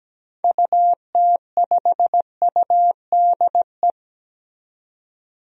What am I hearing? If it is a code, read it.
Morse code, heard as a keyed tone, about 17 wpm, 700 Hz. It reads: UT5UDE
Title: DXCC-100 Call Signs - 6 Characters - 17wpm